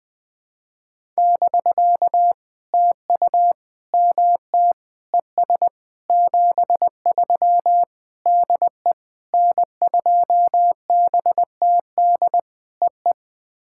Morse code, heard as a keyed tone, about 20 wpm, 700 Hz. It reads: <BK> TU MT ES 73 DE N2BTD EE
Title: POTA QSO Practice - 20wpm